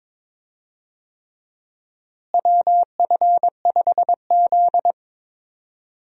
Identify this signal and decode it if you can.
Morse code, heard as a keyed tone, about 22 wpm, 700 Hz. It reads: WF5Z
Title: USA Call Signs - 4 Characters - Repeated 3x - 22wpm